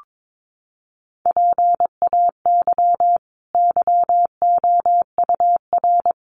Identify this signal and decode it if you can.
Morse code, heard as a keyed tone, about 22 wpm, 700 Hz. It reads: PAY YOUR
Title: Sentences from Top 900 Words 22wpm